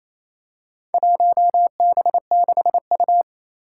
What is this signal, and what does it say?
Morse code, heard as a keyed tone, about 27 wpm, 700 Hz. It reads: 1B6U
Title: Four Challenging Letter-Number Combinations 28wpm